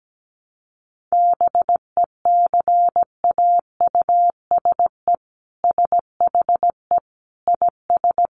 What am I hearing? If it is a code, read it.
Morse code, heard as a keyed tone, about 17 wpm, 700 Hz. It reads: BECAUSE SHE IS
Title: Sentences from Top 1100 Words 17wpm